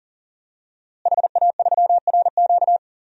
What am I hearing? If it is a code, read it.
Morse code, heard as a keyed tone, about 39 wpm, 700 Hz. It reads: HA3RQ
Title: DXCC-100 Call Signs - 5 Characters - Repeated 2x - 40wpm